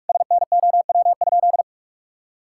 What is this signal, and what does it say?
Morse code, heard as a keyed tone, about 46 wpm, 700 Hz. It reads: SNOW?
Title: Sentences from Top 1600 Words 45wpm